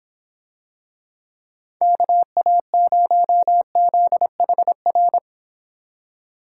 Morse code, heard as a keyed tone, about 26 wpm, 700 Hz. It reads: KA0ZHR